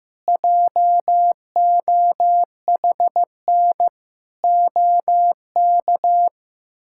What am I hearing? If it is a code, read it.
Morse code, heard as a keyed tone, about 16 wpm, 700 Hz. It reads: JOHN OK
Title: K1USN SST QSO Practice - 15wpm